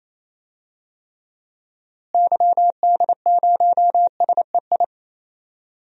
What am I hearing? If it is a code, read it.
Morse code, heard as a keyed tone, about 28 wpm, 700 Hz. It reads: YD0SEI
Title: DXCC-100 Call Signs - 6 Characters - Repeated 2x - 28wpm